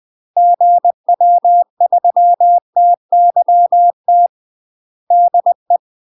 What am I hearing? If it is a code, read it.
Morse code, heard as a keyed tone, about 20 wpm, 700 Hz. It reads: GW3TYT DE
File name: UK MASTER QSOs TRAINING/20WPM